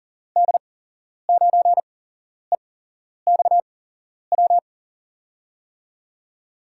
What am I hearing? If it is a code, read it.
Morse code, heard as a keyed tone, about 40 wpm, spaced as for 11 wpm, 700 Hz. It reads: D9EXW
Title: USA Call Signs - 6 Characters - 8x Character Spacing - 40wpm